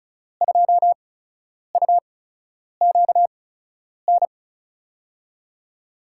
Morse code, heard as a keyed tone, about 36 wpm, spaced as for 10 wpm, 700 Hz. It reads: 2UQN
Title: USA Call Signs - 6 Characters - 8x Character Spacing - 35wpm